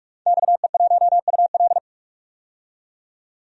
Morse code, heard as a keyed tone, about 46 wpm, 700 Hz. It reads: XE1UL
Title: DXCC-100 Call Signs - 5 Characters - 45wpm